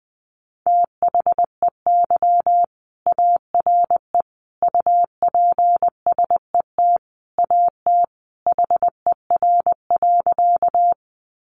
Morse code, heard as a keyed tone, about 20 wpm, 700 Hz. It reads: THEY ARE UPSET AT HER.
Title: Sentences from Top 800 Words 20wpm